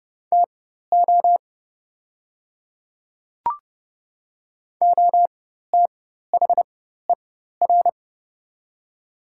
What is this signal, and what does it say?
Morse code, heard as a keyed tone, about 31 wpm, spaced as for 14 wpm, 690 Hz. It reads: TO OTHER
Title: 4x Character Spacing - Top 100 Words - 30wpm